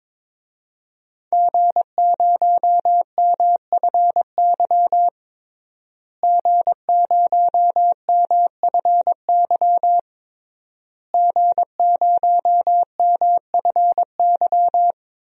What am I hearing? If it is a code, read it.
Morse code, heard as a keyed tone, about 21 wpm, 700 Hz. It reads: G0MFY G0MFY G0MFY
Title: DXCC-100 Call Signs - 5 Characters - Repeated 3x - 22wpm